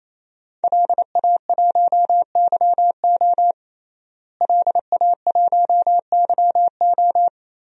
Morse code, heard as a keyed tone, about 27 wpm, 700 Hz. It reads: LA1YO LA1YO
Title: DXCC-100 Call Signs - 5 Characters - Repeated 3x - 28wpm